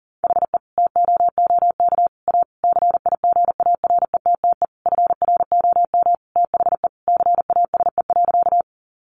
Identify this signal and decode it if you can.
Morse code, heard as a keyed tone, about 40 wpm, 700 Hz. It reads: HE TOOK A CIGARETTE FROM THE CASE.
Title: Sentences from Top 1000 Words 40wpm